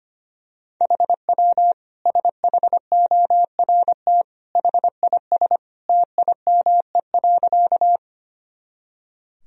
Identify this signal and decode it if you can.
Morse code, heard as a keyed tone, about 25 wpm, 700 Hz. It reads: HW SHORT HIS TIME.
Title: Sentences from Top 400 Words and Abbreviations 25wpm